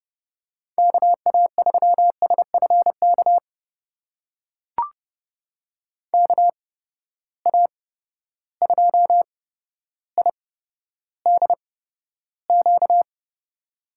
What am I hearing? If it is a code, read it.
Morse code, heard as a keyed tone, about 30 wpm, spaced as for 13 wpm, 700 Hz. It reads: KA3SFK KA2IDQ
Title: USA Call Signs - 6 Characters - 8x Character Spacing - 30wpm